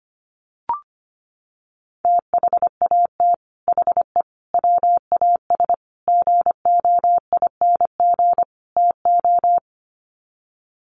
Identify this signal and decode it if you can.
Morse code, heard as a keyed tone, about 25 wpm, 700 Hz. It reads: THAT HE WAS GOING TO
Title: Sets of 5 Words 25wpm